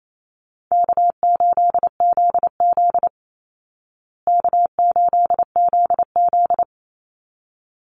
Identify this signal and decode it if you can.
Morse code, heard as a keyed tone, about 28 wpm, 700 Hz. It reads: K8ZZ K8ZZ
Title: Easy Call Signs Repeated 2x - 28wpm